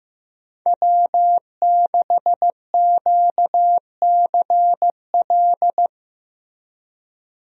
Morse code, heard as a keyed tone, about 15 wpm, 700 Hz. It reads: W6QCL
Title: USA Call Signs - 5 Characters - Repeated 3x - 15wpm